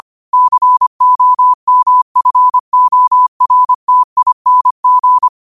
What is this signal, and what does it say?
Morse code, heard as a keyed tone, about 25 wpm, 1000 Hz. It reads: COMFORTING
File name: qotdfiles / 25wpm